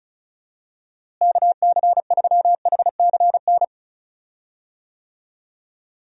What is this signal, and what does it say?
Morse code, heard as a keyed tone, about 36 wpm, 700 Hz. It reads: KC3HCN
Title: USA Call Signs - 6 Characters - Repeated 2x - 35wpm